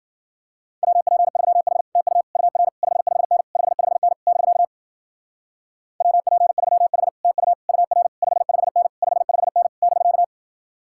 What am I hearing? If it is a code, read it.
Morse code, heard as a keyed tone, about 52 wpm, 700 Hz. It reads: WP3H TU UR 55N 55N <BK> WP3H TU UR 55N 55N <BK>